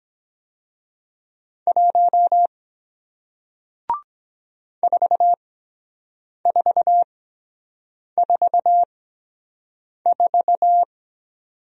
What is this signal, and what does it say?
Morse code, heard as a keyed tone, about 23 wpm, spaced as for 7 wpm, 700 Hz. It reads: 1 4444